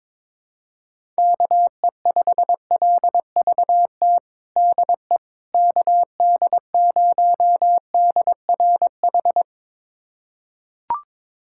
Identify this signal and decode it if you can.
Morse code, heard as a keyed tone, about 22 wpm, 700 Hz. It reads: KE5LVT DE KD0DRH
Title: USA Callsign DE Callsign - 6 Characters - 22wpm